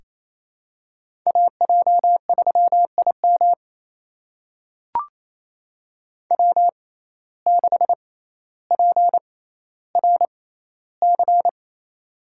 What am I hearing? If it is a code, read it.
Morse code, heard as a keyed tone, about 28 wpm, spaced as for 14 wpm, 700 Hz. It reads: AJ3IM W6PRC
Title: USA Call Signs - 5 Characters - 6x Character Spacing - 28wpm